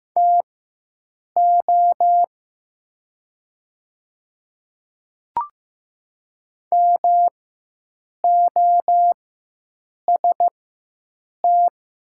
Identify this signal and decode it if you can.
Morse code, heard as a keyed tone, about 15 wpm, spaced as for 7 wpm, 700 Hz. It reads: TO MOST
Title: 4x Character Spacing - Top 100 Words - 15wpm